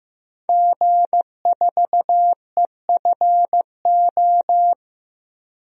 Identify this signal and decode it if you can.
Morse code, heard as a keyed tone, about 15 wpm, 700 Hz. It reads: G4EFO
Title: DXCC-100 Call Signs - 5 Characters - 15wpm